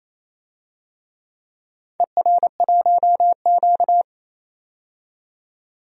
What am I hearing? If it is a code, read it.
Morse code, heard as a keyed tone, about 29 wpm, 700 Hz. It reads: ER1Q
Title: DXCC-100 Call Signs - 4 Characters - 28wpm